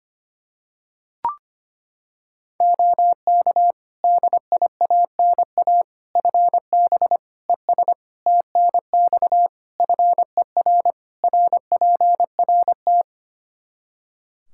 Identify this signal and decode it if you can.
Morse code, heard as a keyed tone, about 25 wpm, 700 Hz. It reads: OK DIANA FB ES TNX FER RPRT
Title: Practice QSO - Exchange 2 - Info - 25wpm